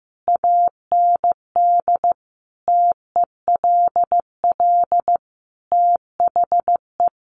Morse code, heard as a keyed tone, about 15 wpm, 700 Hz. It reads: AND TELL THE
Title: Sentences from Top 1200 Words 15wpm